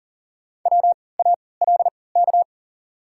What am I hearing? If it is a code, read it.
Morse code, heard as a keyed tone, about 40 wpm, spaced as for 23 wpm, 700 Hz. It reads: WALK
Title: 3x Character Spacing - Top 300 Words - 40wpm